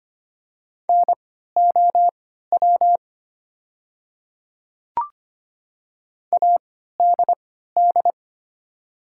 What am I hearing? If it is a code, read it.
Morse code, heard as a keyed tone, about 25 wpm, spaced as for 14 wpm, 700 Hz. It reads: NOW ADD
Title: 3x Character Spacing - Top 200 Words - 25wpm